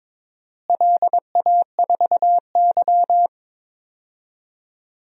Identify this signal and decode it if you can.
Morse code, heard as a keyed tone, about 22 wpm, 700 Hz. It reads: LA4Y
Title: DXCC-100 Call Signs - 4 Characters - Repeated 3x - 22wpm